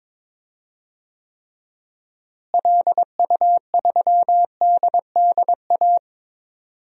Morse code, heard as a keyed tone, about 22 wpm, 700 Hz. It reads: LU3DDA
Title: DXCC-100 Call Signs - 6 Characters - Repeated 2x - 22wpm